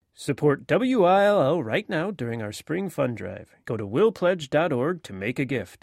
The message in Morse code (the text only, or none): none